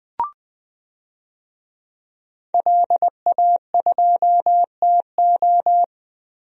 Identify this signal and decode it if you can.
Morse code, heard as a keyed tone, about 20 wpm, 700 Hz. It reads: LA2TO
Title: DXCC-100 Call Signs - 5 Characters - Repeated 2x - 20wpm